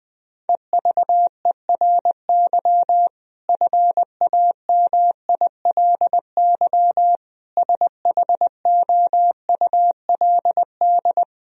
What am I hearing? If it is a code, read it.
Morse code, heard as a keyed tone, about 20 wpm, 700 Hz. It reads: EVERY FAMILY SHOULD
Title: Sentences from Top 1000 Words - Review Part 1 - 20wpm